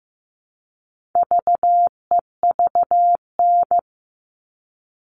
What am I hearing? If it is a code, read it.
Morse code, heard as a keyed tone, about 15 wpm, 700 Hz. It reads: VEVN
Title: All Cut Numbers - 4-Digit 15wpm